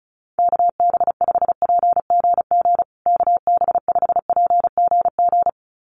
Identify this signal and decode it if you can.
Morse code, heard as a keyed tone, about 34 wpm, 700 Hz. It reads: KB5PGG KB5PGG
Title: USA Call Signs Repeated 2x 35wpm